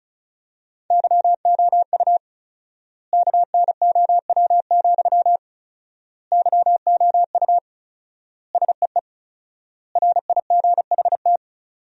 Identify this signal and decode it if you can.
Morse code, heard as a keyed tone, about 36 wpm, 700 Hz. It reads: YOU KNOW, YOU SEE RIGHT
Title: 4x Word-Spacing - Sentences from Top 200 Words - 35wpm